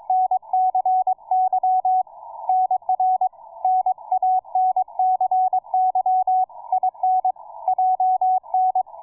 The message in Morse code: NCY NR NANCY IN JN